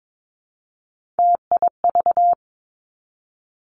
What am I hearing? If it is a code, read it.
Morse code, heard as a keyed tone, about 22 wpm, 700 Hz. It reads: TIV
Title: Three Challenging Letter-Number Combinations 22wpm